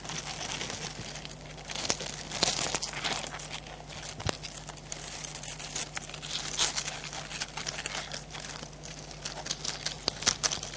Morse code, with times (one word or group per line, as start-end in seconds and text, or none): none